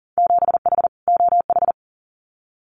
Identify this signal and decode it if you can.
Morse code, heard as a keyed tone, about 37 wpm, 700 Hz. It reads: 7H OH
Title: Winter Field Day Exchanges Encore 40wpm